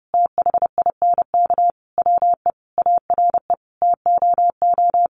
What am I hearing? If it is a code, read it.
Morse code, heard as a keyed tone, about 30 wpm, 700 Hz. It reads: THINK WE ARE TOO
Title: Sentences from Top 1400 Words 30wpm